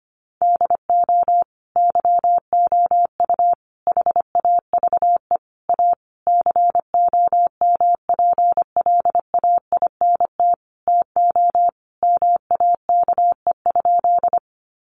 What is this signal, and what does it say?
Morse code, heard as a keyed tone, about 25 wpm, 700 Hz. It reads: DO YOU HAVE A COMPLAINT TO MAKE?
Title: Sentences from Top 1400 Words 25wpm